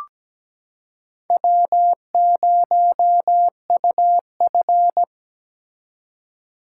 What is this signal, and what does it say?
Morse code, heard as a keyed tone, about 17 wpm, 700 Hz. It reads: W0UF
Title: K1USN SST QSO Practice - 17wpm